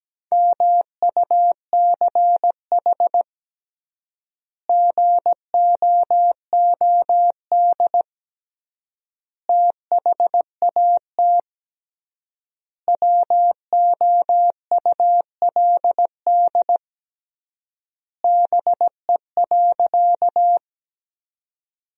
Morse code, heard as a keyed tone, about 17 wpm, 700 Hz. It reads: MUCH GOOD THAT WOULD BE.
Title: 3x Word-Spacing - Sentences from Top 200 Words - 17wpm